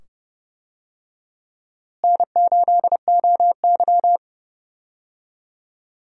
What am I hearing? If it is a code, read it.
Morse code, heard as a keyed tone, about 30 wpm, 700 Hz. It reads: N8OY
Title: USA Call Signs - 4 Characters - Repeated 2x - 30wpm